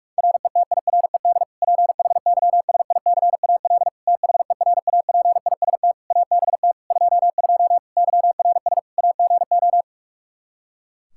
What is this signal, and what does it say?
Morse code, heard as a keyed tone, about 46 wpm, 700 Hz. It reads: RETIRED PHYSICAL THERAPIST ABT 22 YRS AGO
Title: Practice QSO - Exchange 3 - More Info - 45wpm